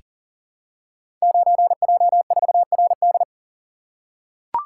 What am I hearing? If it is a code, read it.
Morse code, heard as a keyed tone, about 39 wpm, 700 Hz. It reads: 9J4RD
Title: Code Groups - 5 Letter:Numbers - 40wpm